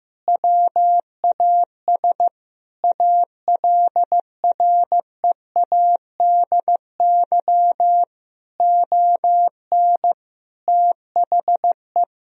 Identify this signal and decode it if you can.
Morse code, heard as a keyed tone, about 15 wpm, 700 Hz. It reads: WAS ALREADY ON THE
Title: Sentences from Top 1000 Words - Review Part 3 - 15wpm